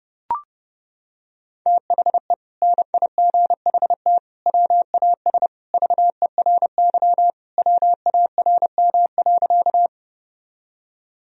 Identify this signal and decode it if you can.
Morse code, heard as a keyed tone, about 32 wpm, 700 Hz. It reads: THE NIGHT WAS VERY WARM.
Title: Sentences from Top 1000 Words - Review Part 2 - 30wpm